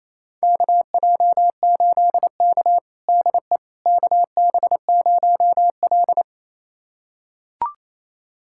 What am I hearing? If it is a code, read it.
Morse code, heard as a keyed tone, about 28 wpm, 700 Hz. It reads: KJ8K DE KB0L
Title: USA Callsign DE Callsign - 4 Characters - 28wpm